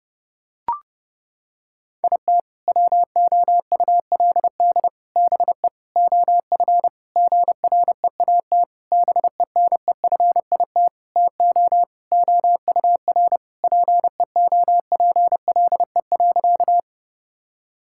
Morse code, heard as a keyed tone, about 30 wpm, 700 Hz. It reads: IT WOULD BE OF GREAT BENEFIT TO OUR PEOPLE.
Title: Sentences from Top 1000 Words - Review Part 6 - 30wpm